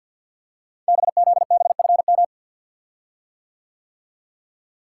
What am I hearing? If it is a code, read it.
Morse code, heard as a keyed tone, about 50 wpm, 690 Hz. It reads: BZBFM
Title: Code Groups - 5 Letters - 50wpm